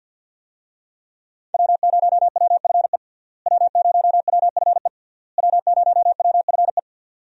Morse code, heard as a keyed tone, about 49 wpm, 700 Hz. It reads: W0WFE W0WFE W0WFE
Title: USA Call Signs - 5 Characters - Repeated 3x - 50wpm